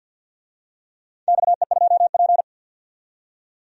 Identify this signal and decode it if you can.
Morse code, heard as a keyed tone, about 49 wpm, 700 Hz. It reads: XE2P
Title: DXCC-100 Call Signs - 4 Characters - 50wpm